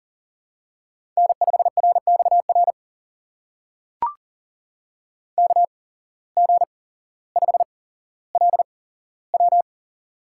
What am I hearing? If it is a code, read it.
Morse code, heard as a keyed tone, about 42 wpm, spaced as for 17 wpm, 700 Hz. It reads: N5RXR KG5LW
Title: USA Call Signs - 5 Characters - 8x Character Spacing - 40wpm